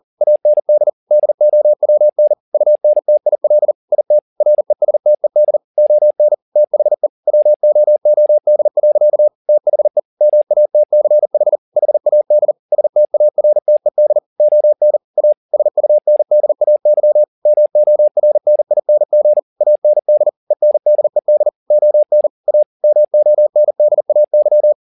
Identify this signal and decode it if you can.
Morse code, heard as a keyed tone, about 40 wpm, 600 Hz. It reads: AND DOWN UNTIL IT RESTED ON THE WOOD. THE MATCH HAD STARTED ON A SUNDAY MORNING AND ENDED ON A MONDAY
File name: the old man and the sea 2 40wpm